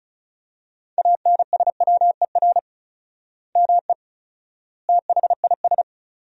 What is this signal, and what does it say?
Morse code, heard as a keyed tone, about 34 wpm, 700 Hz. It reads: ANSWER ME THIS
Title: 4x Word-Spacing - Sentences from Top 400 Words - 35wpm